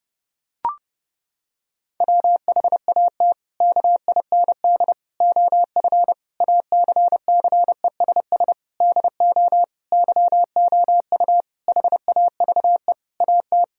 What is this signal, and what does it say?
Morse code, heard as a keyed tone, about 32 wpm, 700 Hz. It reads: WHAT KIND OF ACCESS DO YOU HAVE AT
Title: Sentences from Top 1000 Words - Review Part 4 - 30wpm